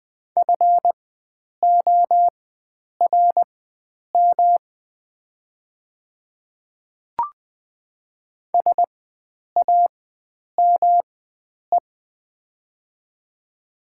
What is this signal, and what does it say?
Morse code, heard as a keyed tone, about 20 wpm, spaced as for 9 wpm, 700 Hz. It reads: FORM SAME
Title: 4x Character Spacing - Top 300 Words - 20wpm